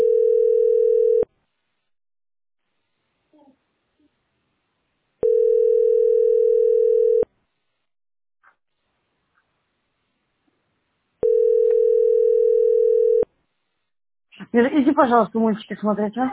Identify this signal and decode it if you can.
Ringback tone (North America), 440 + 480 Hz, with speech.
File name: f5a1f35c-c269-4ac7-a916